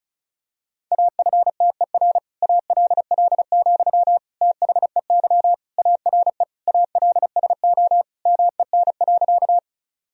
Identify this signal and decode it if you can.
Morse code, heard as a keyed tone, about 36 wpm, 700 Hz. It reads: AFTER ALL, THEY ARE ALSO MEN.
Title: Sentences from Top 1000 Words - Review Part 1 - 35wpm